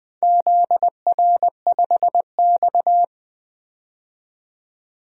Morse code, heard as a keyed tone, about 21 wpm, 700 Hz. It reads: ZR5X
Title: DXCC-100 Call Signs - 4 Characters - 20wpm